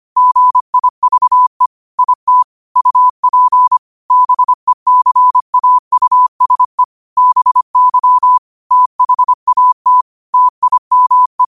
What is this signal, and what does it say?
Morse code, heard as a keyed tone, about 25 wpm, 1000 Hz. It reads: GIVE IT UP BECAUSE BY THAT TIME